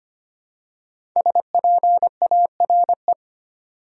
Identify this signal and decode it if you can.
Morse code, heard as a keyed tone, about 25 wpm, 700 Hz. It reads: SPARE
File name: 5 Letter Words - First 3 Letters Given - 25wpm